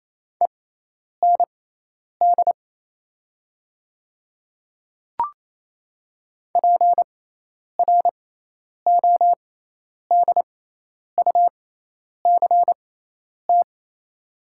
Morse code, heard as a keyed tone, about 28 wpm, spaced as for 10 wpm, 700 Hz. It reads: END PRODUCT
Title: 6x Character Spacing - Top 500 Words - 28wpm